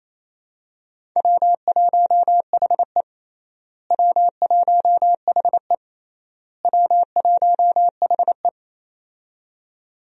Morse code, heard as a keyed tone, about 28 wpm, 700 Hz. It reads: W1HE W1HE W1HE
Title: USA Call Signs - 4 Characters - Repeated 3x - 28wpm